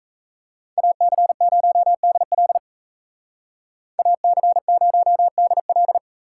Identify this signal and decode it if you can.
Morse code, heard as a keyed tone, about 33 wpm, 700 Hz. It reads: AC0DL AC0DL
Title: Speed Racing - USA Call Signs 28wpm